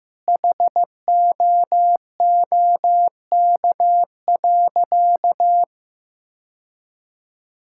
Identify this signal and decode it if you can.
Morse code, heard as a keyed tone, about 15 wpm, 700 Hz. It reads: HOOK.
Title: Sentences from Top 1600 Words 15wpm